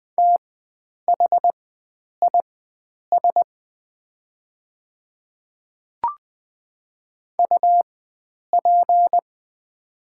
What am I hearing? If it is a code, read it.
Morse code, heard as a keyed tone, about 20 wpm, spaced as for 9 wpm, 700 Hz. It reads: THIS UP
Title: 4x Character Spacing - Top 300 Words - 20wpm